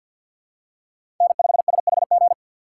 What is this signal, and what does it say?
Morse code, heard as a keyed tone, about 47 wpm, 700 Hz. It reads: N5SHG
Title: USA Call Signs - 3x Character Spacing - 50wpm